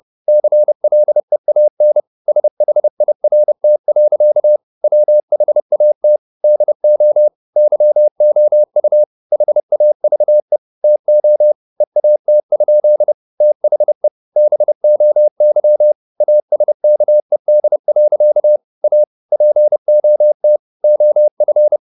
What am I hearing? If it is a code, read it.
Morse code, heard as a keyed tone, about 30 wpm, 600 Hz. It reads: CLEAN SHIRT. WHAT DO YOU HAVE TO EAT? THE BOY ASKED. A POT OF
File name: the old man and the sea / the old man and the sea 1 30wpm